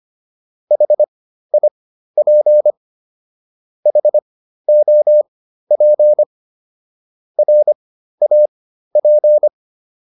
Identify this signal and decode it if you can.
Morse code, heard as a keyed tone, about 25 wpm, spaced as for 13 wpm, 600 Hz. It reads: HIP HOP RAP
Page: intermediate-practice-files